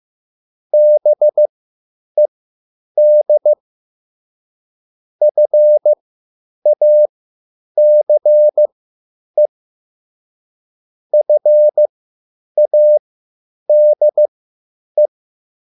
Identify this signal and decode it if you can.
Morse code, heard as a keyed tone, about 15 wpm, spaced as for 9 wpm, 600 Hz. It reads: BED FACE FADE